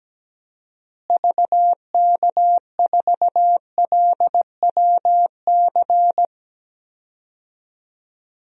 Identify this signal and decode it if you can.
Morse code, heard as a keyed tone, about 17 wpm, 700 Hz. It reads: VK4LWC
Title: DXCC-100 Call Signs - 6 Characters - Repeated 3x - 17wpm